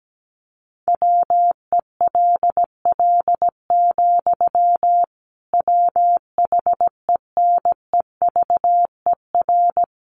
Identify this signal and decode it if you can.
Morse code, heard as a keyed tone, about 17 wpm, 700 Hz. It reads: WELL, WHENEVER
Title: Sentences from Top 1200 Words 17wpm